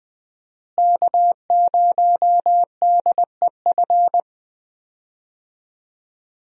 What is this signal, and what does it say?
Morse code, heard as a keyed tone, about 20 wpm, 700 Hz. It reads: K0DEF
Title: USA Call Signs - 5 Characters - Repeated 2x - 20wpm